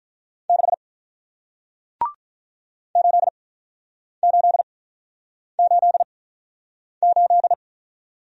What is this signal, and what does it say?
Morse code, heard as a keyed tone, about 48 wpm, spaced as for 9 wpm, 700 Hz. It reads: 6 8888